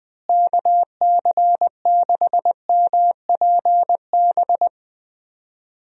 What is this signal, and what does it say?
Morse code, heard as a keyed tone, about 20 wpm, 700 Hz. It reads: KC6MPB
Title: USA Call Signs - 6 Characters - Repeated 3x - 20wpm